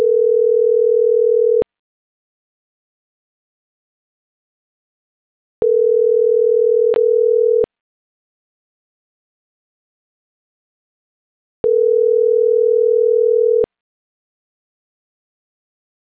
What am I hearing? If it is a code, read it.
Ringback tone (North America), 440 + 480 Hz.